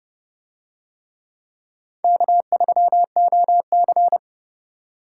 Morse code, heard as a keyed tone, about 29 wpm, 700 Hz. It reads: K3OC